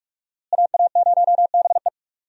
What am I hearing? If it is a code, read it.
Morse code, heard as a keyed tone, about 44 wpm, 700 Hz. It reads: AA0BE
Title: USA Call Signs - 5 Characters - Repeated 3x - 45wpm